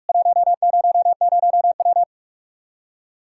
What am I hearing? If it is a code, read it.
Morse code, heard as a keyed tone, about 46 wpm, 700 Hz. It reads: 100W
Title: Practice QSO - Exchange 2 - Info - 45wpm